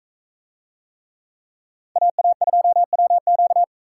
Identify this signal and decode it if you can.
Morse code, heard as a keyed tone, about 40 wpm, 700 Hz. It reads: AA2WQ